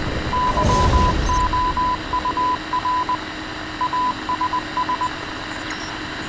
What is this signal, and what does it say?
Morse code, heard as a keyed tone, about 20 wpm, 1000 Hz, with radio noise. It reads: YOUR ASS